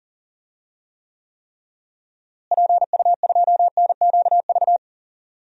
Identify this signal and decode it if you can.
Morse code, heard as a keyed tone, about 40 wpm, 700 Hz. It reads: PU2NQV
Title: DXCC-100 Call Signs - 6 Characters - Repeated 3x - 40wpm